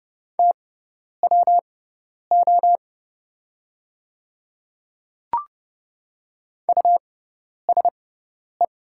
Morse code, heard as a keyed tone, about 30 wpm, spaced as for 10 wpm, 700 Hz. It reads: TWO USE
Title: 6x Character Spacing - Top 500 Words - 30wpm